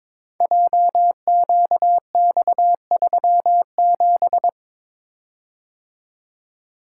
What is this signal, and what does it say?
Morse code, heard as a keyed tone, about 22 wpm, 700 Hz. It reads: JQX37
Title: Code Groups - 5 Letter:Numbers - 22wpm